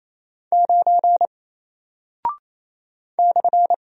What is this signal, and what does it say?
Morse code, heard as a keyed tone, about 28 wpm, 700 Hz. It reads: 9 /
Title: Single Character (Letter:Number:Punctuation:Prosign) Alt 28wpm